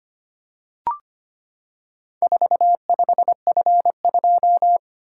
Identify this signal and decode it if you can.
Morse code, heard as a keyed tone, about 25 wpm, 700 Hz. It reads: 45F2
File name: practice / 25wpm / Four Letter-Number Combinations - 25wpm